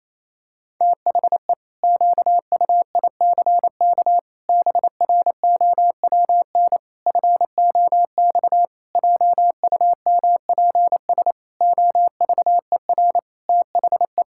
Morse code, heard as a keyed tone, about 27 wpm, 700 Hz. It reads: THE QUICK BROWN FOX JUMPS OVER THE
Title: Pangram Challenge v2 - 28wpm